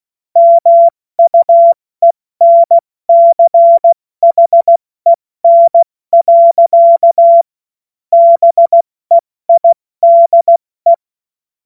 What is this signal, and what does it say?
Morse code, heard as a keyed tone, about 16 wpm, 680 Hz. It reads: MUENCHEN. BEIDE
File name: Morsekurs / Witze / Witz 07.November 2023 70 F80